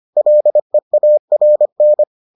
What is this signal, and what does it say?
Morse code, heard as a keyed tone, about 25 wpm, 600 Hz. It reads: LEARN